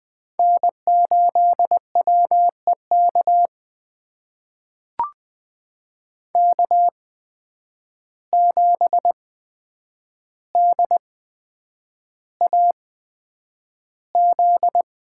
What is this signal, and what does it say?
Morse code, heard as a keyed tone, about 20 wpm, spaced as for 9 wpm, 700 Hz. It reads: N8WEK K7DAZ
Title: USA Call Signs - 5 Characters - 8x Character Spacing - 20wpm